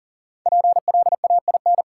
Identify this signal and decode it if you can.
Morse code, heard as a keyed tone, about 40 wpm, 700 Hz. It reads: PLAIN
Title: Sentences from Top 400 Words - New Word Drill - 40wpm